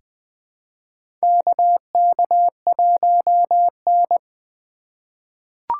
Morse code, heard as a keyed tone, about 21 wpm, 700 Hz. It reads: KK1N
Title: USA Call Signs - 2x Character Spacing - 20wpm